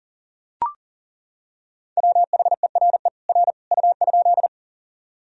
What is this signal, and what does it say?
Morse code, heard as a keyed tone, about 40 wpm, 700 Hz. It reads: WHERE R U?
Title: Sentences from Top 200 Words and Abbreviations 40wpm